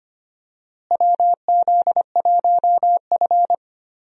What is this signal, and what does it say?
Morse code, heard as a keyed tone, about 25 wpm, 700 Hz. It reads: WZ1F
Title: USA Call Signs - 4 Characters - Repeated 3x - 25wpm